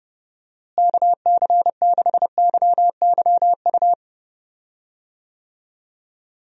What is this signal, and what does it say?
Morse code, heard as a keyed tone, about 30 wpm, 700 Hz. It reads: KC6YYU
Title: USA Call Signs - 6 Characters - Repeated 2x - 30wpm